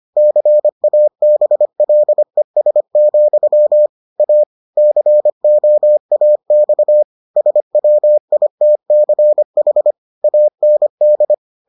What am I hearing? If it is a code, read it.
Morse code, heard as a keyed tone, about 25 wpm, 600 Hz. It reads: CABLES, A COAX SWITCH AND